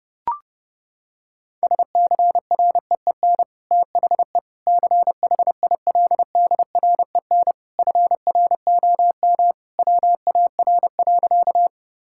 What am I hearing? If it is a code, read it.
Morse code, heard as a keyed tone, about 30 wpm, 700 Hz. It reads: SCREEN THE CHILDREN FROM WAR.
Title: Sentences from Top 1000 Words - Review Part 6 - 30wpm